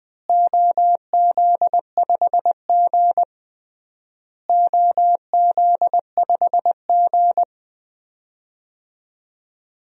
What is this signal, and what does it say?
Morse code, heard as a keyed tone, about 20 wpm, 700 Hz. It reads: OZ5G OZ5G
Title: DXCC-100 Call Signs - 4 Characters - Repeated 3x - 20wpm